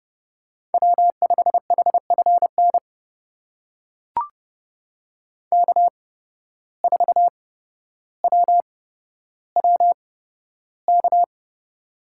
Morse code, heard as a keyed tone, about 31 wpm, spaced as for 13 wpm, 700 Hz. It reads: W5HFN K4WWK
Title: USA Call Signs - 5 Characters - 8x Character Spacing - 30wpm